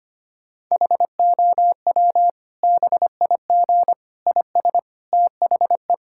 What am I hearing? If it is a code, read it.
Morse code, heard as a keyed tone, about 25 wpm, 700 Hz. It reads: HOW BIG IS THE
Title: Sentences from Top 1000 Words - Review Part 5 - 25wpm